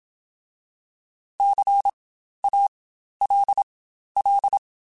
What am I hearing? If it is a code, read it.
Morse code, heard as a keyed tone, about 27 wpm, spaced as for 12 wpm, 800 Hz. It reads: CALL